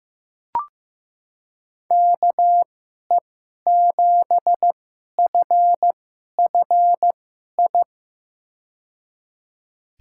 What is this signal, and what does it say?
Morse code, heard as a keyed tone, about 15 wpm, spaced as for 11 wpm, 700 Hz. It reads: KE7FFI
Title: USA Call Signs - 6 Characters - 2x Character Spacing - 15wpm